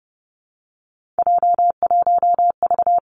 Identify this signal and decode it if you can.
Morse code, heard as a keyed tone, about 30 wpm, 700 Hz. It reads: J1V
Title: Three Challenging Letter-Number Combinations 30wpm